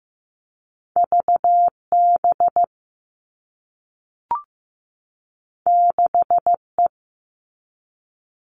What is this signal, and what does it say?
Morse code, heard as a keyed tone, about 15 wpm, 700 Hz. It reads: VB 6E
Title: All Cut Numbers - 2-Digit 15wpm